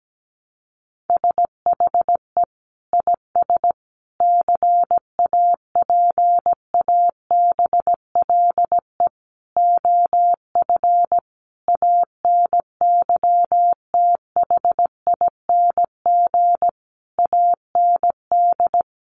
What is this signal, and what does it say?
Morse code, heard as a keyed tone, about 17 wpm, 700 Hz. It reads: SHE IS CAPABLE OF ANYTHING AND
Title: Sentences from Top 1200 Words 17wpm